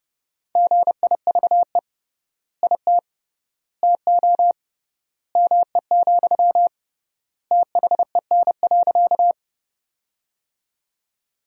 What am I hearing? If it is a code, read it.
Morse code, heard as a keyed tone, about 30 wpm, 700 Hz. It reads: GIVE IT TO ME, THEN.
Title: 3x Word-Spacing - Sentences from Top 200 Words - 30wpm